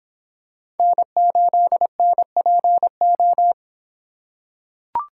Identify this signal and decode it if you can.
Morse code, heard as a keyed tone, about 25 wpm, 700 Hz. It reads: N8NPO